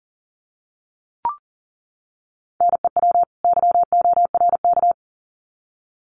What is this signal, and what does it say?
Morse code, heard as a keyed tone, about 40 wpm, 700 Hz. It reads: NEW YORK
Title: Sets of 2 Words 40wpm